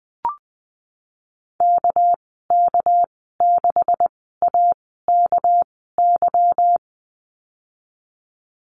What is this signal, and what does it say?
Morse code, heard as a keyed tone, about 20 wpm, spaced as for 15 wpm, 700 Hz. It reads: KK6AKY
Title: 200 Lesson 32g - Letter k - Call Signs 20wpm